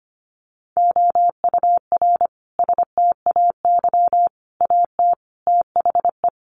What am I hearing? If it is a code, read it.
Morse code, heard as a keyed tone, about 25 wpm, 700 Hz. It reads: OUR STAY AT THE
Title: Sentences from Top 1400 Words 25wpm